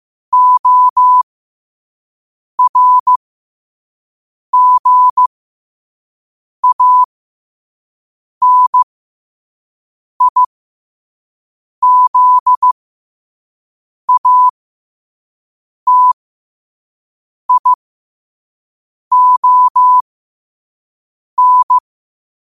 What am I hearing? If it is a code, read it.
Morse code, heard as a keyed tone, about 15 wpm, spaced as for 5 wpm, 1000 Hz. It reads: ORGANIZATION
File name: qotdfiles / 5wpm